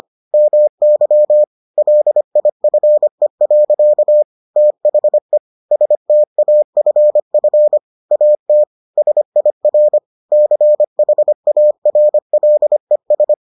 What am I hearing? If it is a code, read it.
Morse code, heard as a keyed tone, about 25 wpm, 600 Hz. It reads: MY LIFE. THE STAFF AT SIR CHARLES